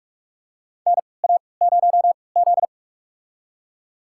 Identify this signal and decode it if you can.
Morse code, heard as a keyed tone, about 46 wpm, spaced as for 26 wpm, 700 Hz. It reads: NA0Z
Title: USA Call Signs - 4 Characters - 3x Character Spacing - 45wpm